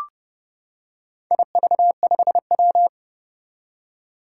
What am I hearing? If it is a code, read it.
Morse code, heard as a keyed tone, about 28 wpm, 700 Hz. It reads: IV5W
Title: Four Challenging Letter-Number Combinations 30wpm